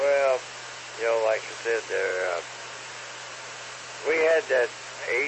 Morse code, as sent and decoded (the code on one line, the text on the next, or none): none